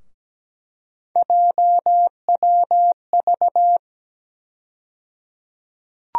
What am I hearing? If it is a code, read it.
Morse code, heard as a keyed tone, about 17 wpm, 700 Hz. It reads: JWV
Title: USA Call Sign Postfixes - 3 Letters 17wpm